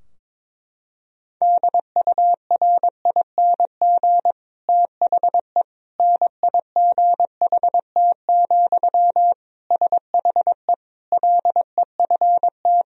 Morse code, heard as a keyed tone, about 22 wpm, 700 Hz. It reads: DURING THE NIGHT, SHE LEFT
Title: Sentences from Top 1000 Words - Review Part 2 - 22wpm